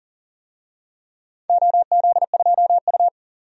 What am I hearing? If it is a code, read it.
Morse code, heard as a keyed tone, about 40 wpm, 700 Hz. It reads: OZ2U
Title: DXCC-100 Call Signs - 4 Characters - 40wpm